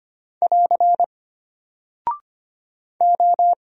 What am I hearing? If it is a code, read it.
Morse code, heard as a keyed tone, about 25 wpm, 700 Hz. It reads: <AR> O
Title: Single Character (Letter:Number:Punctuation:Prosign) Alt 25wpm